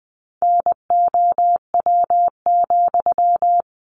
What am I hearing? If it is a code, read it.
Morse code, heard as a keyed tone, about 19 wpm, 700 Hz. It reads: NOW,
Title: Sentences from Top 800 Words 20wpm